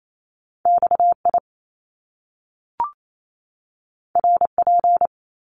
Morse code, heard as a keyed tone, about 28 wpm, 700 Hz. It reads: XI RP
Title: Top Two Letter Combinations in Words 28wpm